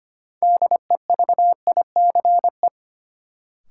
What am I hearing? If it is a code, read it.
Morse code, heard as a keyed tone, about 25 wpm, 700 Hz. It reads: DEVICE
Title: Sentences from Top 1500 Words - New Word Drill - 25wpm